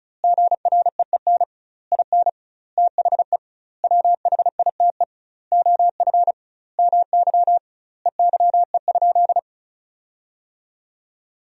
Morse code, heard as a keyed tone, about 34 wpm, 700 Hz. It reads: GREEN IN THE WHITE OF MY EYE?
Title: 2x Word-Spacing - Sentences from Top 500 Words - 35wpm